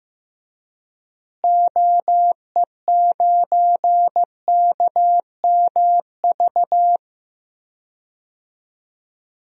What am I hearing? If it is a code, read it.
Morse code, heard as a keyed tone, about 15 wpm, 700 Hz. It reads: OE9KMV
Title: DXCC-100 Call Signs - 6 Characters - Repeated 3x - 15wpm